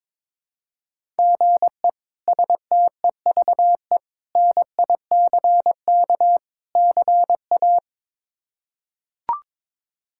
Morse code, heard as a keyed tone, about 22 wpm, 700 Hz. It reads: GE STEVE NICK CA
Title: K1USN SST QSO Practice - 22wpm